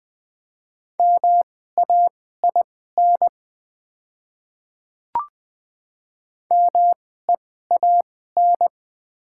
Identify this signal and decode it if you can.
Morse code, heard as a keyed tone, about 20 wpm, spaced as for 15 wpm, 700 Hz. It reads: MAIN MEAN
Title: 2x Character Spacing - Top 300 Words - 20wpm